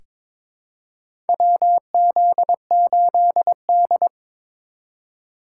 Morse code, heard as a keyed tone, about 23 wpm, 700 Hz. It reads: WZ8D
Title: USA Call Signs - 4 Characters - 4x Character Spacing - 22wpm